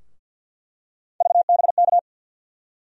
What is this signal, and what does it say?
Morse code, heard as a keyed tone, about 49 wpm, 700 Hz. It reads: VBK
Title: USA Call Sign Postfixes - 3 Letters 50wpm